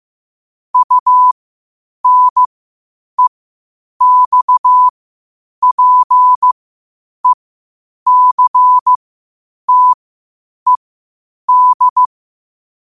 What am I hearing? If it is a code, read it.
Morse code, heard as a keyed tone, about 15 wpm, spaced as for 8 wpm, 1000 Hz. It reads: UNEXPECTED
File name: qotdfiles / 8wpm